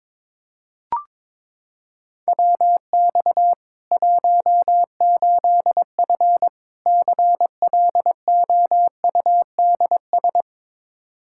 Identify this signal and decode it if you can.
Morse code, heard as a keyed tone, about 22 wpm, 700 Hz. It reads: WX 18F CLOUDS
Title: Practice QSO - Exchange 2 - Info - 22wpm